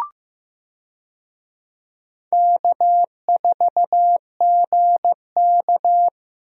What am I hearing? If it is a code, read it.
Morse code, heard as a keyed tone, about 14 wpm, 700 Hz. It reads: K4GK
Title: USA Call Signs - 4 Characters - Repeated 3x - 15wpm